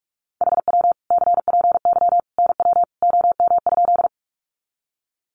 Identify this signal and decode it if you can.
Morse code, heard as a keyed tone, about 44 wpm, 700 Hz. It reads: HW CPY NW OM?
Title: Common QSO Phrases 45wpm